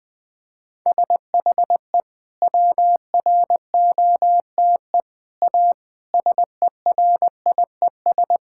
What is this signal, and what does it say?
Morse code, heard as a keyed tone, about 20 wpm, 700 Hz. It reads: SHE WROTE A SERIES
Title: Sentences with 3-Word Phrases - Spoken Beginning - 20wpm